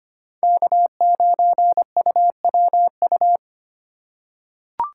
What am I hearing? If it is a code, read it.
Morse code, heard as a keyed tone, about 25 wpm, 700 Hz. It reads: K9UWU
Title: USA Call Signs - 5 Characters - 2x Character Spacing - 25wpm